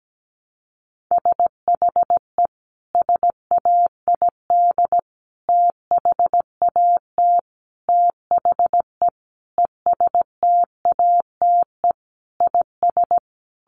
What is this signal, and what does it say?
Morse code, heard as a keyed tone, about 17 wpm, 700 Hz. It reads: SHE SAID THAT THE ESTATE IS
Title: Sentences from Top 1200 Words 17wpm